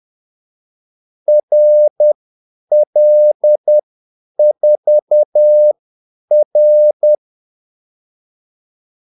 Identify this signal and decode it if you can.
Morse code, heard as a keyed tone, about 10 wpm, spaced as for 8 wpm, 600 Hz. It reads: RL4R
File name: Callsigns-10-8wpm